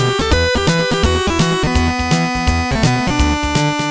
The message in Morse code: none